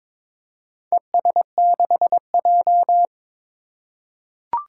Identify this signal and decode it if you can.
Morse code, heard as a keyed tone, about 22 wpm, 700 Hz. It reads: ES6J
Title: Four Challenging Letter-Number Combinations 22wpm